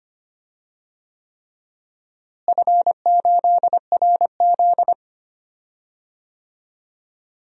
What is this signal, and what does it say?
Morse code, heard as a keyed tone, about 25 wpm, 700 Hz. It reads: F8RZ
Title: DXCC-100 Call Signs - 4 Characters - Repeated 3x - 25wpm